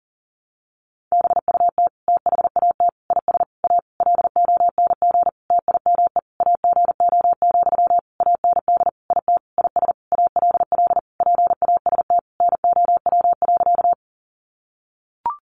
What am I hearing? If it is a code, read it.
Morse code, heard as a keyed tone, about 40 wpm, 700 Hz. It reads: BUT THAT IS A LONG TIME AGO, AND IT IS ALL PAST NOW.
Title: Sentences from Top 600 Words 40wpm